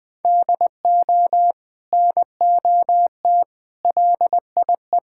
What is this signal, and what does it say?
Morse code, heard as a keyed tone, about 20 wpm, 700 Hz. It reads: DO NOT LIE
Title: Sentences from Top 1600 Words 20wpm